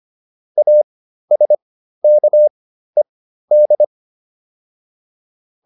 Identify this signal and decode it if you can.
Morse code, heard as a keyed tone, about 25 wpm, spaced as for 13 wpm, 600 Hz. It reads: ASKED